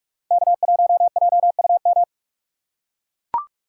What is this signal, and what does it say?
Morse code, heard as a keyed tone, about 44 wpm, 700 Hz. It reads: K1JUM